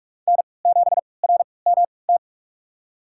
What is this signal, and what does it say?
Morse code, heard as a keyed tone, about 43 wpm, spaced as for 25 wpm, 700 Hz. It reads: N7RMT
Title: USA Call Signs - 3x Character Spacing - 45wpm